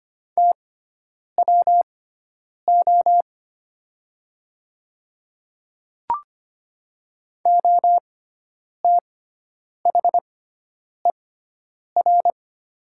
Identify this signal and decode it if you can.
Morse code, heard as a keyed tone, about 25 wpm, spaced as for 9 wpm, 700 Hz. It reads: TWO OTHER
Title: 6x Character Spacing - Top 100 Words - 25wpm